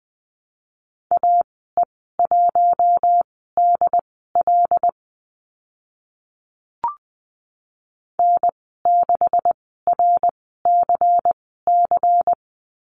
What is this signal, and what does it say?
Morse code, heard as a keyed tone, about 20 wpm, spaced as for 14 wpm, 700 Hz. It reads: AE1DL N6RCC
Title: 128 Lesson 21g - Number 6 - Call Signs 20wpm